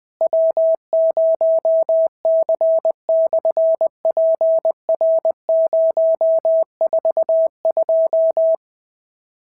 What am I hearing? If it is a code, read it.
Morse code, heard as a keyed tone, about 20 wpm, 650 Hz. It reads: W0C/PR042
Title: SOTA QSO Practice 20wpm